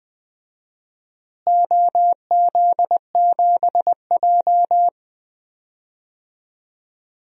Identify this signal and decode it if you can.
Morse code, heard as a keyed tone, about 20 wpm, 700 Hz. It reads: OZ7J
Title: DXCC-100 Call Signs - 4 Characters - Repeated 2x - 20wpm